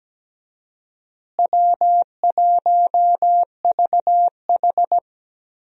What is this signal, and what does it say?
Morse code, heard as a keyed tone, about 17 wpm, 700 Hz. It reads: W1VH
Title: USA Call Signs - 4 Characters - Repeated 3x - 17wpm